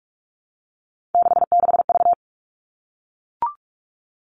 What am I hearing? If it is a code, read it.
Morse code, heard as a keyed tone, about 44 wpm, 700 Hz. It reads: 66V
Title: All Cut Numbers - 3-Digit 45wpm